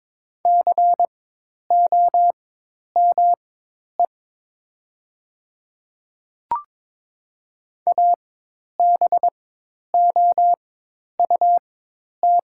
Morse code, heard as a keyed tone, about 22 wpm, spaced as for 10 wpm, 700 Hz. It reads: COME ABOUT
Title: 4x Character Spacing - Top 100 Words - 22wpm